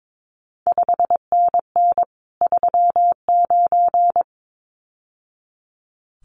Morse code, heard as a keyed tone, about 22 wpm, 700 Hz. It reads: 5NN 39
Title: CQ WW DX Contest Exchange - 22wpm